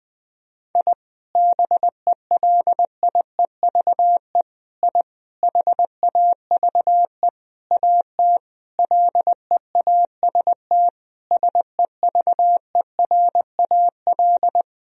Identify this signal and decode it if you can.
Morse code, heard as a keyed tone, about 20 wpm, 700 Hz. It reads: I BELIEVE I HAVE AT LEAST SEVERAL
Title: Sentences from Top 1000 Words - Review Part 2 - 20wpm